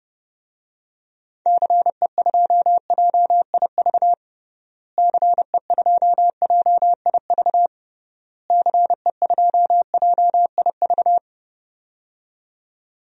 Morse code, heard as a keyed tone, about 30 wpm, 700 Hz. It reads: CE2JIV CE2JIV CE2JIV
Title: DXCC-100 Call Signs - 6 Characters - Repeated 3x - 30wpm